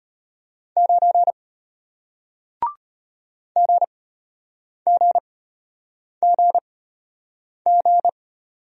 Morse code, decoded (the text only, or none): 9 GGGG